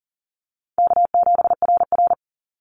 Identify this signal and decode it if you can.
Morse code, heard as a keyed tone, about 40 wpm, 700 Hz. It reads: K7RR
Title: Call Signs Repeated 3x - 40wpm